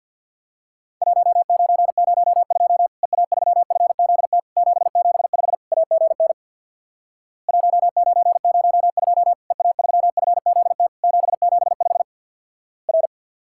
Morse code, heard as a keyed tone, about 50 wpm, 700 Hz. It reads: 1902 EA3FZT 775 AGN 1902 EA3FZT 775 R
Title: Worked All Europe DX Contest - QTC Exchange with Repeats - 50wpm